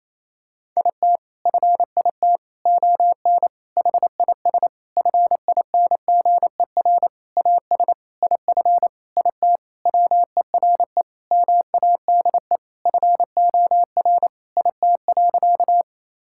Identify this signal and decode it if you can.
Morse code, heard as a keyed tone, about 28 wpm, 700 Hz. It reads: IT FIT ON HIS FINGER AS IF IT WERE MADE FOR IT.
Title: Sentences from Top 1600 Words 28wpm